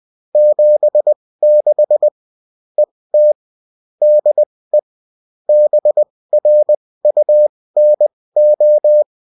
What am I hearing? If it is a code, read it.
Morse code, heard as a keyed tone, about 20 wpm, spaced as for 16 wpm, 600 Hz. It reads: 76 ET DE BRUNO